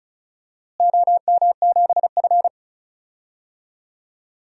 Morse code, heard as a keyed tone, about 34 wpm, 700 Hz. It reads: OM7F
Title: DXCC-100 Call Signs - 4 Characters - Repeated 2x - 35wpm